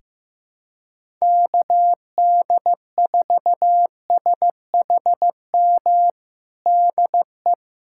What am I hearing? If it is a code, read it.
Morse code, heard as a keyed tone, about 15 wpm, 700 Hz. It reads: KD4SHM DE
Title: USA Callsign DE Callsign - 6 Characters - 15wpm